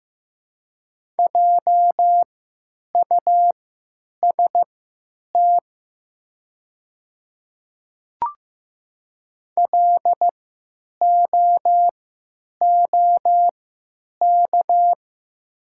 Morse code, heard as a keyed tone, about 15 wpm, spaced as for 9 wpm, 700 Hz. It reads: JUST LOOK
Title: 3x Character Spacing - Top 200 Words - 15wpm